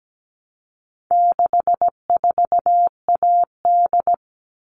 Morse code, heard as a keyed tone, about 17 wpm, 700 Hz. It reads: 64AD
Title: All Cut Numbers - 4-Digit 17wpm